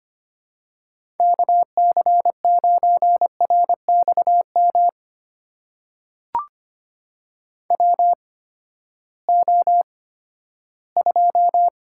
Morse code, heard as a keyed tone, about 25 wpm, spaced as for 14 wpm, 700 Hz. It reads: KC9RXM WO2
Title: USA Call Signs - 8x Character Spacing - 25wpm